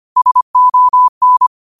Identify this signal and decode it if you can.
Morse code, heard as a keyed tone, about 25 wpm, 1000 Hz. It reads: SON
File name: qotdfiles / 25wpm